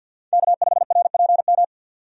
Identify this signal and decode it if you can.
Morse code, heard as a keyed tone, about 47 wpm, 700 Hz. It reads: K5RPM
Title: USA Call Signs - 5 Characters - 50wpm